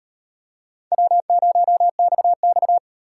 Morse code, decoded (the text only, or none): W0XX